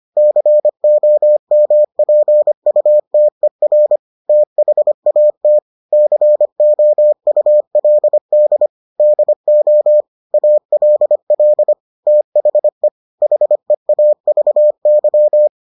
Morse code, heard as a keyed tone, about 25 wpm, 600 Hz. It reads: COMPUTER THAT COULD DO ALL THE HEAVY